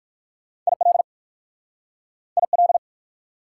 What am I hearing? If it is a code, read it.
Morse code, heard as a keyed tone, about 58 wpm, 700 Hz. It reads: IL IL